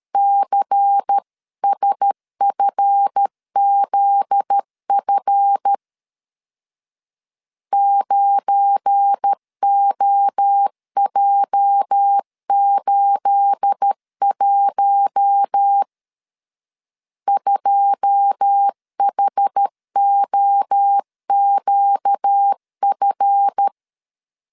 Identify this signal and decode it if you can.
Morse code, heard as a keyed tone, about 13 wpm, 800 Hz. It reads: CSFZF 9OJ81 2HOQF
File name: CW / DET / 1 9 1500C S A 12wpm 3800